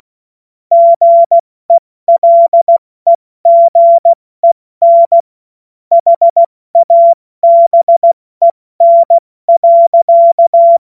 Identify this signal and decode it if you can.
Morse code, heard as a keyed tone, about 16 wpm, 680 Hz. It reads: GELEGEN HABEN.
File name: Morsekurs/Witze/Witz 12 Dezember 2023 70 F80